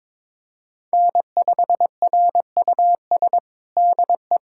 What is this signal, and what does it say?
Morse code, heard as a keyed tone, about 22 wpm, 700 Hz. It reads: N5RUS DE
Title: USA Callsign DE Callsign - 5 Characters - 22wpm